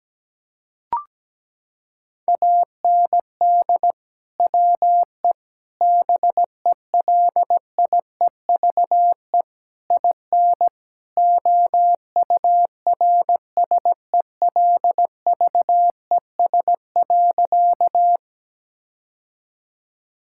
Morse code, heard as a keyed tone, about 17 wpm, 700 Hz. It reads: AND WE BELIEVE IN OURSELVES.
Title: Sentences from Top 1600 Words 17wpm